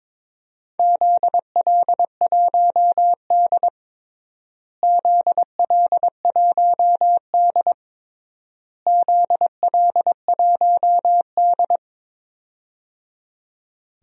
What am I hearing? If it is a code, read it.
Morse code, heard as a keyed tone, about 22 wpm, 700 Hz. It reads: ZL1D ZL1D ZL1D
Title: DXCC-100 Call Signs - 4 Characters - Repeated 3x - 22wpm